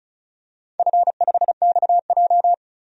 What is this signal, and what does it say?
Morse code, heard as a keyed tone, about 36 wpm, 700 Hz. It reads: F5XJ
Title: DXCC-100 Call Signs - 4 Characters - 35wpm